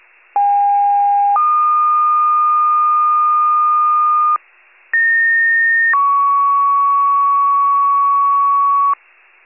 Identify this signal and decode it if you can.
Two-tone page, heard as a one-second tone, then a three-second tone, about 810 then 1220 Hz, about 1850 then 1120 Hz. It is heard from 0.4 to 8.9 s.